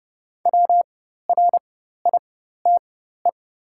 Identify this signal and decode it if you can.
Morse code, heard as a keyed tone, about 30 wpm, spaced as for 14 wpm, 690 Hz. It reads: WRITE
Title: 4x Character Spacing - Top 500 Words - 30wpm